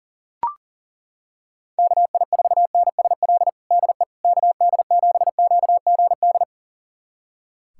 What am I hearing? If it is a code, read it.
Morse code, heard as a keyed tone, about 40 wpm, 700 Hz. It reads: KI4NSL DE KD7QGD
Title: USA Callsign DE Callsign - 6 Characters - 40wpm